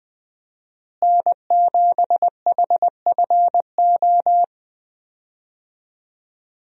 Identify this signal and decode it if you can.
Morse code, heard as a keyed tone, about 20 wpm, 700 Hz. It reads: N7HFO
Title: USA Call Signs - 5 Characters - Repeated 3x - 20wpm